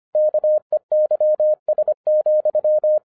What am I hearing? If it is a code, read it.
Morse code, heard as a keyed tone, about 25 wpm, 610 Hz. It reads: KEYS,
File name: 2011-10-mp3 / n14-MORSE EXPRESS 2011 CHRISTMAS KEY 25wpm